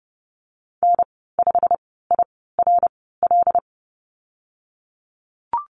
Morse code, heard as a keyed tone, about 30 wpm, spaced as for 17 wpm, 700 Hz. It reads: N5IRL
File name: Lesson 19g - Letter w - Call Signs 30wpm